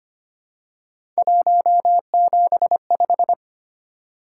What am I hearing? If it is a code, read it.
Morse code, heard as a keyed tone, about 25 wpm, 700 Hz. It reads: 175
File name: Three Letter-Number Combinations - 25wpm